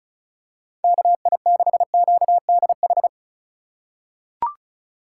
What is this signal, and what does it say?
Morse code, heard as a keyed tone, about 36 wpm, 700 Hz. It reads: KI6QDH
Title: K1USN SST QSO Practice - 35wpm